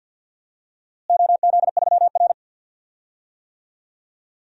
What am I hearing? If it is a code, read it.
Morse code, heard as a keyed tone, about 52 wpm, 700 Hz. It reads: OZ3R
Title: DXCC-100 Call Signs - 4 Characters - Repeated 2x - 50wpm